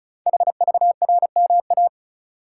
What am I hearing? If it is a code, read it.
Morse code, heard as a keyed tone, about 36 wpm, 690 Hz. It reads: HVRMA
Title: Code Groups - 5 Letters - 35wpm